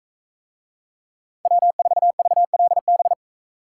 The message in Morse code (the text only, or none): W4VLB